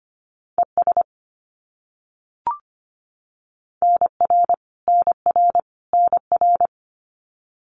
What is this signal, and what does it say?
Morse code, heard as a keyed tone, about 25 wpm, 700 Hz. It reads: ES NR NR NR
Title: Abbreviations Repeated 3x - 25wpm